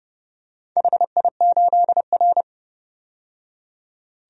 Morse code, heard as a keyed tone, about 28 wpm, 700 Hz. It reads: HI8R
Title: DXCC-100 Call Signs - 4 Characters - 30wpm